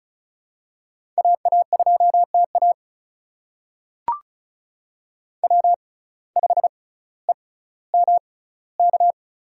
Morse code, heard as a keyed tone, about 36 wpm, spaced as for 18 wpm, 700 Hz. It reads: AA2TA W5EMK
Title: USA Call Signs - 5 Characters - 6x Character Spacing - 35wpm